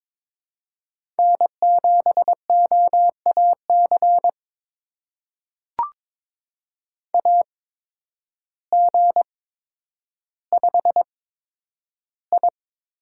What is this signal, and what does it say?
Morse code, heard as a keyed tone, about 22 wpm, spaced as for 10 wpm, 700 Hz. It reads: N7OAC AG5I
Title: USA Call Signs - 8x Character Spacing - 22wpm